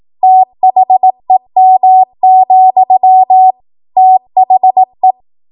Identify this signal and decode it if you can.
Morse code, heard as a keyed tone, about 18 wpm, 750 Hz. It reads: THEM, THE
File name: Archive / 18 WPM / 18WPM